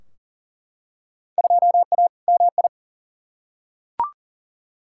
Morse code, heard as a keyed tone, about 40 wpm, 700 Hz. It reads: 2A MI
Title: Field Day Exchanges - 40wpm